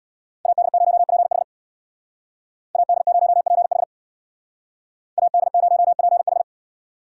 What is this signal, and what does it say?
Morse code, heard as a keyed tone, about 62 wpm, 700 Hz. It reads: AD0PH AD0PH AD0PH
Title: Speed Racing - USA 2x2 Call Signs 50wpm